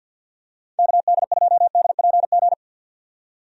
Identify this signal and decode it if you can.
Morse code, heard as a keyed tone, about 50 wpm, 700 Hz. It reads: KD2DPG
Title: USA Call Signs - 6 Characters - 50wpm